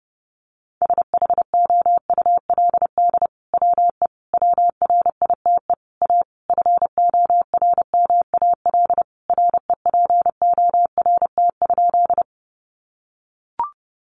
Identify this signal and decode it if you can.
Morse code, heard as a keyed tone, about 30 wpm, 700 Hz. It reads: SHOULD WE WRITE A FORMAL REPORT?
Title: Sentences from Top 1300 Words 30wpm